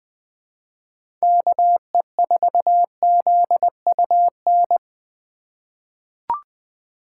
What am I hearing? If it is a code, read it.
Morse code, heard as a keyed tone, about 20 wpm, 700 Hz. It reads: KE4ZUN
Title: 2023 POTA Hunter Call Signs - 20wpm